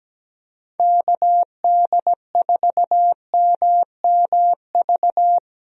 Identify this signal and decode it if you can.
Morse code, heard as a keyed tone, about 17 wpm, 700 Hz. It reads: KD4MMV
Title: USA Call Signs - 6 Characters - Repeated 3x - 17wpm